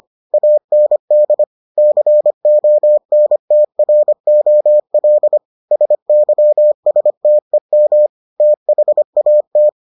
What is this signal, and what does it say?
Morse code, heard as a keyed tone, about 25 wpm, 600 Hz. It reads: AND CONTROL SYSTEM THAT